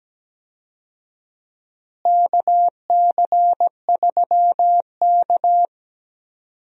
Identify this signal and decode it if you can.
Morse code, heard as a keyed tone, about 17 wpm, 700 Hz. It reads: KC3K
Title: USA Call Signs - 4 Characters - Repeated 2x - 17wpm